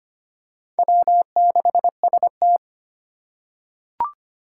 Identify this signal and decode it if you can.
Morse code, heard as a keyed tone, about 25 wpm, 700 Hz. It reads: W6ST
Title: USA Call Signs - 4x Character Spacing - 25wpm